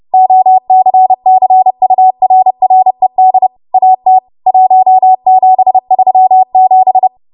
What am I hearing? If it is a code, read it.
Morse code, heard as a keyed tone, about 30 wpm, 750 Hz. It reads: OCCURRED AT 1737
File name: Morse / 30WPM